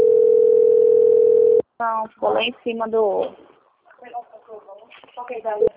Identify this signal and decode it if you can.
Ringback tone (North America), 440 + 480 Hz, with speech.
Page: Recife - PE